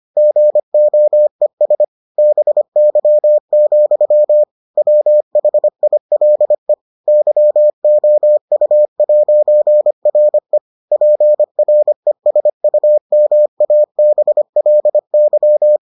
Morse code, heard as a keyed tone, about 25 wpm, 600 Hz. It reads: GOES BY, WHILE YOU'RE PRESUMABLY